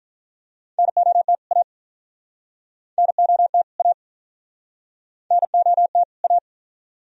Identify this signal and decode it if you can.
Morse code, heard as a keyed tone, about 50 wpm, 690 Hz. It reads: NOT A NOT A NOT A